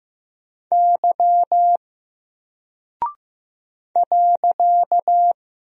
Morse code, heard as a keyed tone, about 15 wpm, 700 Hz. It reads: Y .
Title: Single Character (Letter:Number:Punctuation:Prosign) 15wpm